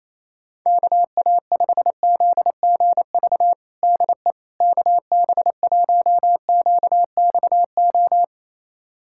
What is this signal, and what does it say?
Morse code, heard as a keyed tone, about 28 wpm, 700 Hz. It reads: KA5ZGV DE KB1QXO
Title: USA Callsign DE Callsign - 6 Characters - 28wpm